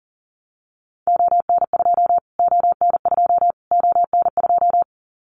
Morse code, heard as a keyed tone, about 40 wpm, 700 Hz. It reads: ON2 ON2 ON2
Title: Callsign Prefixes for Top 25 DXCC - Repeated 3x - 40wpm